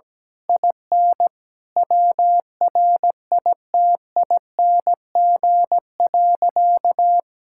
Morse code, heard as a keyed tone, about 17 wpm, 700 Hz. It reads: IN WRITING.
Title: Sentences from Top 1600 Words 17wpm